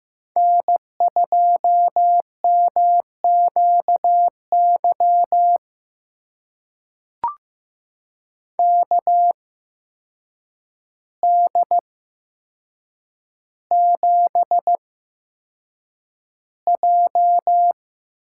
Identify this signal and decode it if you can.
Morse code, heard as a keyed tone, about 15 wpm, spaced as for 7 wpm, 700 Hz. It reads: N2MQY KD7J
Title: USA Call Signs - 5 Characters - 8x Character Spacing - 15wpm